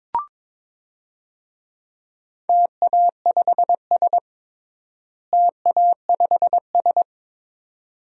Morse code, heard as a keyed tone, about 22 wpm, 700 Hz. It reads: TA5S TA5S
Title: DXCC-100 Call Signs - 4 Characters - Repeated 2x - 22wpm